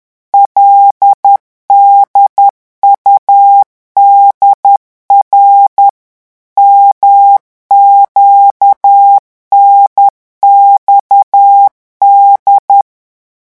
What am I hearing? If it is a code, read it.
Morse code, heard as a keyed tone, about 10 wpm, 800 Hz. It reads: LDUDR MQNXD
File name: morse mp3 / revision5-vitesse 10 mots